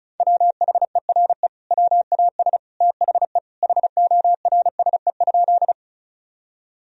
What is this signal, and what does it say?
Morse code, heard as a keyed tone, about 36 wpm, 700 Hz. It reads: WHERE WAS THE HORSE?
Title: Sentences from Top 1000 Words - Review Part 1 - 35wpm